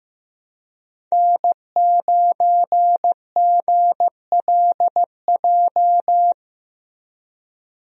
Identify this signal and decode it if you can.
Morse code, heard as a keyed tone, about 15 wpm, 700 Hz. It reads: N9GLJ
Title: USA Call Signs - 5 Characters - Repeated 3x - 15wpm